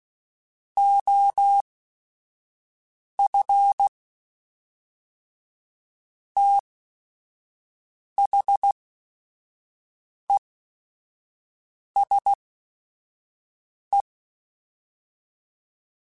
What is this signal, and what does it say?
Morse code, heard as a keyed tone, about 16 wpm, spaced as for 5 wpm, 800 Hz. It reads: OF THESE
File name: Technical Subjects / Technical Subjects-mp3s / Reactance @5